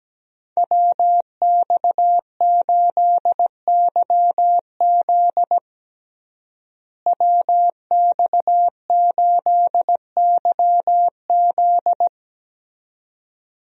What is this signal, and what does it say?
Morse code, heard as a keyed tone, about 17 wpm, 700 Hz. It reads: WX8YZ WX8YZ
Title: USA Call Signs - 5 Characters - Repeated 2x - 17wpm